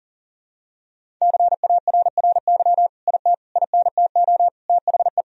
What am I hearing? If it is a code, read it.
Morse code, heard as a keyed tone, about 40 wpm, 700 Hz. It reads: CARRY IT INTO THE
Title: Sentences from Top 1000 Words - Review Part 1 - 40wpm